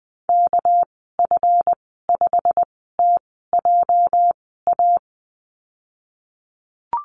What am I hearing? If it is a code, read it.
Morse code, heard as a keyed tone, about 19 wpm, spaced as for 14 wpm, 700 Hz. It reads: KF5TJA
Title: 238 Lesson 38g - Letter q - Call Signs 20wpm